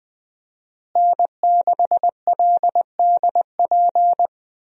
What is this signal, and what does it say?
Morse code, heard as a keyed tone, about 20 wpm, 700 Hz. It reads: N6LDP
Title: USA Call Signs - 5 Characters - Repeated 2x - 20wpm